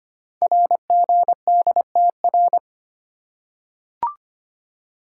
Morse code, heard as a keyed tone, about 25 wpm, 700 Hz. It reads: RGDTR
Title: Code Groups - 5 Letters - 25wpm